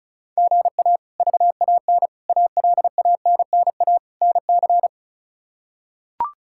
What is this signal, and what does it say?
Morse code, heard as a keyed tone, about 34 wpm, 700 Hz. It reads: GA VAN ALANNA NC
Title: K1USN SST QSO Practice - 35wpm